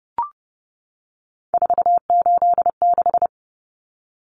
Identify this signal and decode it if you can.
Morse code, heard as a keyed tone, about 30 wpm, 700 Hz. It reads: 486
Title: Common Cut Numbers - 3-Digit 30wpm